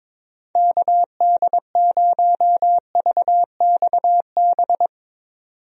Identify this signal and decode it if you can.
Morse code, heard as a keyed tone, about 21 wpm, 700 Hz. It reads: KD0VXB
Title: USA Call Signs - 6 Characters - 22wpm